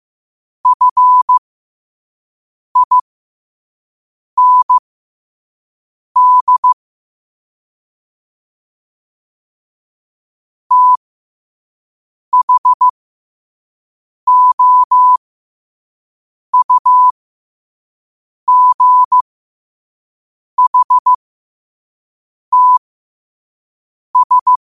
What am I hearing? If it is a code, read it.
Morse code, heard as a keyed tone, about 15 wpm, spaced as for 5 wpm, 1000 Hz. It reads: FIND THOUGHTS
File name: qotdfiles / 5wpm